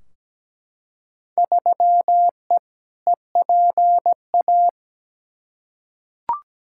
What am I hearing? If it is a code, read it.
Morse code, heard as a keyed tone, about 17 wpm, 700 Hz. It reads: 3E EPA
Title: Field Day Exchanges Encore - 17wpm